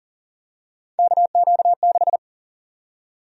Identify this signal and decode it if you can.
Morse code, heard as a keyed tone, about 40 wpm, 700 Hz. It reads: KQ6
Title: USA Call Sign Prefixes - Repeated 2x - 40wpm